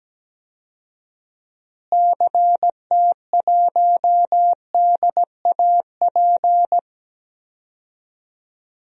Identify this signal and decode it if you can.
Morse code, heard as a keyed tone, about 17 wpm, 700 Hz. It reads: CT1DAP
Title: DXCC-100 Call Signs - 6 Characters - Repeated 3x - 17wpm